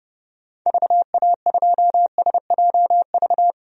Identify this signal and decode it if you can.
Morse code, heard as a keyed tone, about 30 wpm, 700 Hz. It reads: VA2SJV
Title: DXCC-100 Call Signs - 6 Characters - Repeated 2x - 30wpm